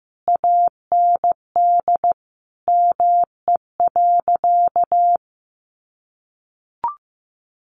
Morse code, heard as a keyed tone, about 14 wpm, 700 Hz. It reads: AND ME.
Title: Sentences from Top 500 Words 15wpm